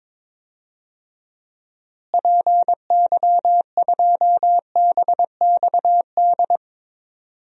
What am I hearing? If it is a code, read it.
Morse code, heard as a keyed tone, about 22 wpm, 700 Hz. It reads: PY2BXD
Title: DXCC-100 Call Signs - 6 Characters - Repeated 2x - 22wpm